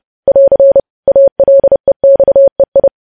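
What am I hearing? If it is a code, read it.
Morse code, heard as a keyed tone, about 30 wpm, 550 Hz. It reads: <AR> ALEXEI
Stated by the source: News headlines sent in Morse code at 30 wpm.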